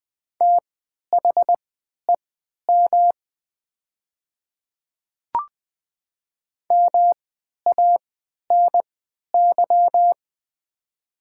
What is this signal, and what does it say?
Morse code, heard as a keyed tone, about 20 wpm, spaced as for 11 wpm, 700 Hz. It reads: THEM MANY
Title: 3x Character Spacing - Top 100 Words - 20wpm